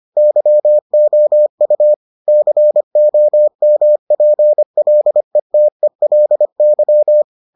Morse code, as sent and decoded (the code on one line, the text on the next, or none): -.-- --- ..- / -.-. --- -- .--. .-.. - . .-.. -.--
YOU COMPLETELY